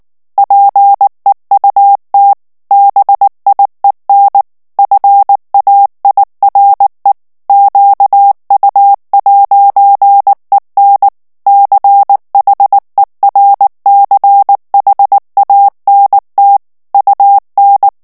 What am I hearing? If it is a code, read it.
Morse code, heard as a keyed tone, about 19 wpm, 800 Hz. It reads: PEUT BIEN FAIRE QU'EN CHERCHANT UN
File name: CW / FICHIERS-MP3 / FRANCAIS TEXTE CLAIR 2 VITESSE 18 MOTS MINUTES